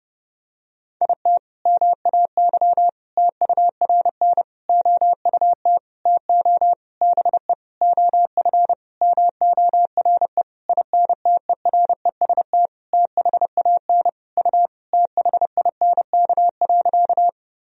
Morse code, heard as a keyed tone, about 30 wpm, 700 Hz. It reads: IT MAY TURN OUT TO BE OF MORE INTEREST THAN U THINK.
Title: Sentences from Top 400 Words and Abbreviations 30wpm